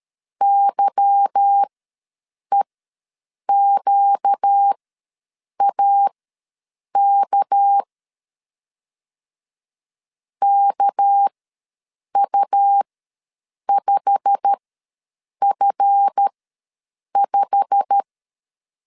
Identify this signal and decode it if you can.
Morse code, heard as a keyed tone, about 13 wpm, spaced as for 7 wpm, 800 Hz. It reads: YEQAK KU5F5
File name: LEZIONI / 8 YJ45 / 8 YJ45 ws 12wpm 3146